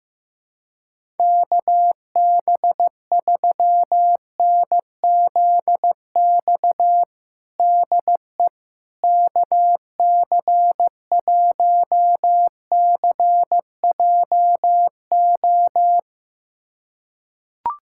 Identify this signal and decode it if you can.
Morse code, heard as a keyed tone, about 15 wpm, 700 Hz. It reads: KB3NZX DE KC1CJO
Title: USA Callsign DE Callsign 15wpm